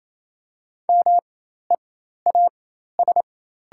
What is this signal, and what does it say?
Morse code, heard as a keyed tone, about 28 wpm, spaced as for 13 wpm, 700 Hz. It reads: MEAS
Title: 4x Character Spacing - Top 500 Words - 28wpm